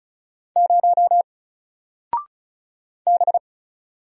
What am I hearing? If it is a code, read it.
Morse code, heard as a keyed tone, about 36 wpm, 700 Hz. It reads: 0 B
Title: Single Letter-Number Alt 35wpm